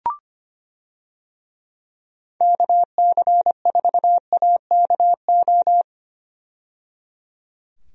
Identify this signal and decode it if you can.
Morse code, heard as a keyed tone, about 25 wpm, 700 Hz. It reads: KC4AKO
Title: USA Call Signs - 6 Characters - 25wpm